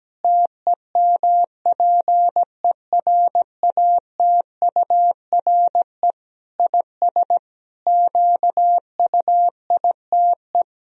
Morse code, heard as a keyed tone, about 17 wpm, 700 Hz. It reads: TEMPERATURE IS QUITE
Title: Sentences from Top 1000 Words - Review Part 4 - 17wpm